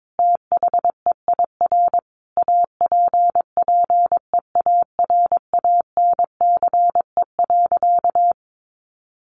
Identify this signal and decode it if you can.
Morse code, heard as a keyed tone, about 22 wpm, 700 Hz. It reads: THEIR APPEARANCE.
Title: Sentences from Top 1100 Words 22wpm